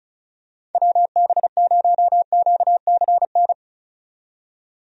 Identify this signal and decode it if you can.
Morse code, heard as a keyed tone, about 36 wpm, 700 Hz. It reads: WB0QCN
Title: USA Call Signs - 6 Characters - 35wpm